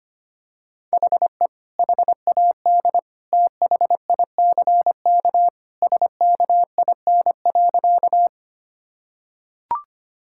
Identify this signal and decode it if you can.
Morse code, heard as a keyed tone, about 25 wpm, 700 Hz. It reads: HE HAD THICK SKIN.
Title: Sentences from Top 1600 Words 25wpm